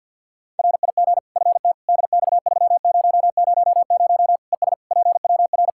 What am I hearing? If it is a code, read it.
Morse code, heard as a keyed tone, about 52 wpm, 700 Hz. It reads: RIG FT DX3000 ES PWR
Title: K1USN SST QSO Practice 50wpm